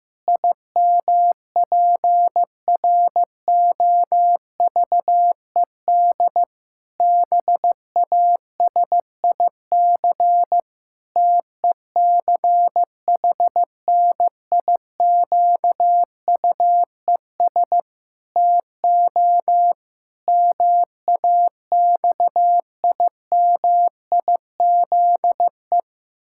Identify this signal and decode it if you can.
Morse code, heard as a keyed tone, about 15 wpm, 700 Hz. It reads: IMPROVED BASIC TECHNIQUES TO MAXIMIZE
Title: Pangram Challenge v2 - 15wpm